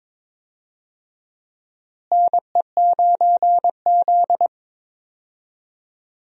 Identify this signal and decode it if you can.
Morse code, heard as a keyed tone, about 22 wpm, 700 Hz. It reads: NE9Z
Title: USA Call Signs - 4 Characters - Repeated 2x - 22wpm